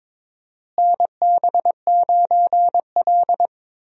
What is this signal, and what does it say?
Morse code, heard as a keyed tone, about 22 wpm, 700 Hz. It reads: NB9L
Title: USA Call Signs - 4 Characters - Repeated 2x - 22wpm